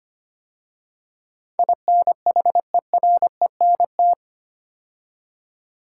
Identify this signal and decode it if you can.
Morse code, heard as a keyed tone, about 25 wpm, 700 Hz. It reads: INHERENT
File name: 8 Letter Words - First 3 Letters Given - 25wpm